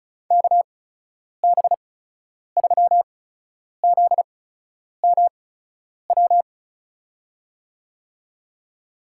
Morse code, heard as a keyed tone, about 36 wpm, spaced as for 10 wpm, 700 Hz. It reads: KB3ZMW
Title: USA Call Signs - 6 Characters - 8x Character Spacing - 35wpm